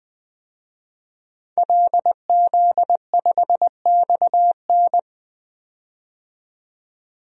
Morse code, heard as a keyed tone, about 20 wpm, 700 Hz. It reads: LZ5XN
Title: DXCC-100 Call Signs - 5 Characters - 20wpm